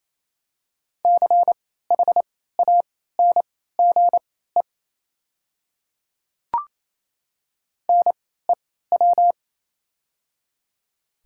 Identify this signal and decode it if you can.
Morse code, heard as a keyed tone, about 28 wpm, spaced as for 16 wpm, 700 Hz. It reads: CHANGE NEW
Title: 3x Character Spacing - Top 200 Words - 28wpm